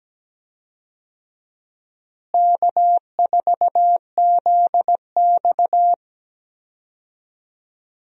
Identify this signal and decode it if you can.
Morse code, heard as a keyed tone, about 17 wpm, 700 Hz. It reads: K4ZX
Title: USA Call Signs - 4 Characters - Repeated 3x - 17wpm